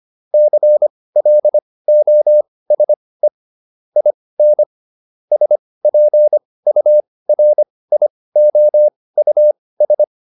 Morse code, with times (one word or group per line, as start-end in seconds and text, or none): 0.3-3.3 CLOSE
4.0-4.6 IN
5.3-10.0 SPURIOUS